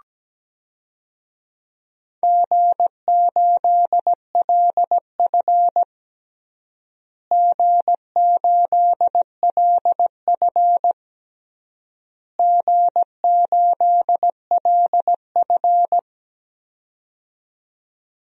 Morse code, heard as a keyed tone, about 17 wpm, 700 Hz. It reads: G8LF G8LF G8LF
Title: DXCC-100 Call Signs - 4 Characters - Repeated 3x - 17wpm